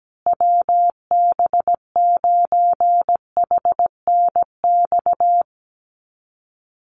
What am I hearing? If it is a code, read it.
Morse code, heard as a keyed tone, about 17 wpm, 700 Hz. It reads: WB9HNX
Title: USA Call Signs Repeated 3x 17wpm